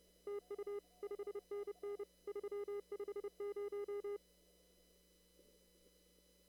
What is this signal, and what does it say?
Morse code, heard as a keyed tone, about 30 wpm, 410 Hz. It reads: TU 5NN 350
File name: CW / kt5j wpx cw-op-k5pi